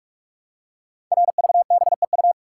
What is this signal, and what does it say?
Morse code, heard as a keyed tone, about 44 wpm, 700 Hz. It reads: RVBEU
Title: Code Groups - 5 Letters - 45wpm